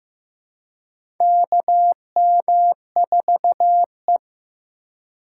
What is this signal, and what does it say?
Morse code, heard as a keyed tone, about 15 wpm, 700 Hz. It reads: KM4E
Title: USA Call Signs - 3x Character Spacing - 15wpm